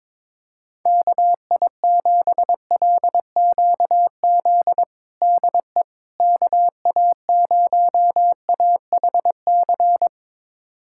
Morse code, heard as a keyed tone, about 22 wpm, 700 Hz. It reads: KI7LQZ DE KA0AHC
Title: USA Callsign DE Callsign 22wpm